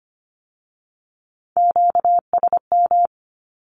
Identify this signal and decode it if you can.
Morse code, heard as a keyed tone, about 25 wpm, 700 Hz. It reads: QSM
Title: Q-Codes with Spoken Meaning - 25wpm